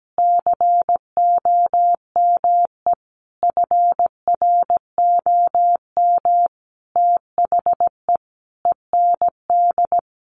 Morse code, heard as a keyed tone, about 17 wpm, 700 Hz. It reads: COME FROM THE END
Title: Sentences from Top 200 Words 17wpm